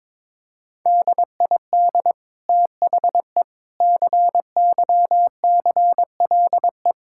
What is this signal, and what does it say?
Morse code, heard as a keyed tone, about 22 wpm, 700 Hz. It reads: DID THE CYCLE
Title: Sentences from Top 1000 Words - Review Part 6 - 22wpm